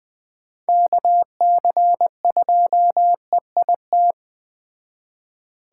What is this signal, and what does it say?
Morse code, heard as a keyed tone, about 21 wpm, 700 Hz. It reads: KC2EIT